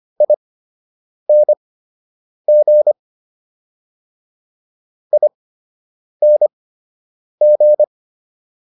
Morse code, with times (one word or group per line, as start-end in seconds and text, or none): 0.2-2.9 ING
5.1-7.8 ING